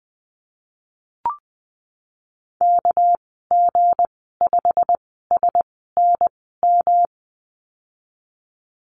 Morse code, heard as a keyed tone, about 20 wpm, 700 Hz. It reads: K G 5 S N M
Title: 200 Lesson 32g - Letter k - Call Signs 20wpm